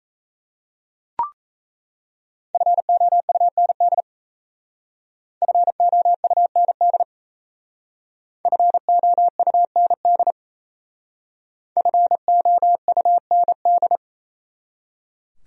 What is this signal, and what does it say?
Morse code, heard as a keyed tone, about 34 wpm, 700 Hz. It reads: FOUND FOUND FOUND FOUND
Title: Speed-Racing Top 300 Words 28wpm